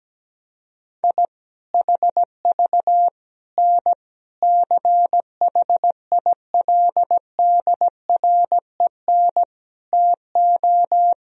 Morse code, heard as a keyed tone, about 17 wpm, 700 Hz. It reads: I HV N CHILDREN TO
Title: Sentences from Top 400 Words and Abbreviations 17wpm